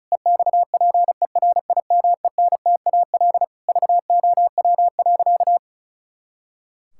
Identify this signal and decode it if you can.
Morse code, heard as a keyed tone, about 36 wpm, 700 Hz. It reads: EXPERIMENTAL VOW.
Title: Pangram Challenge v2 - 35wpm